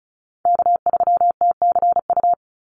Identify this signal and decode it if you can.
Morse code, heard as a keyed tone, about 34 wpm, 700 Hz. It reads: K3TCU
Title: NA SOTA Chaser Callsigns 35wpm